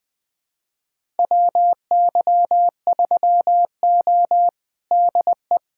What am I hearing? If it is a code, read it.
Morse code, heard as a keyed tone, about 20 wpm, 700 Hz. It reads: WY3O DE
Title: USA Callsign DE Callsign - 4 Characters - 20wpm